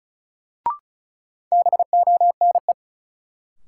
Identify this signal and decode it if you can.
Morse code, heard as a keyed tone, about 34 wpm, 700 Hz. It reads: BONE
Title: Sentences from Top 1600 Words - New Word Drill - 35wpm